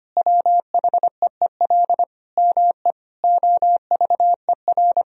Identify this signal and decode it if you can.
Morse code, heard as a keyed tone, about 25 wpm, 700 Hz. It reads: WHEEL ME OVER
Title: Sentences from Top 1000 Words - Review Part 3 - 25wpm